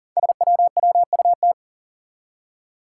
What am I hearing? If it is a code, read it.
Morse code, heard as a keyed tone, about 42 wpm, 700 Hz. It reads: SWWUT
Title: Five Challenging Letter-Number Combinations 40wpm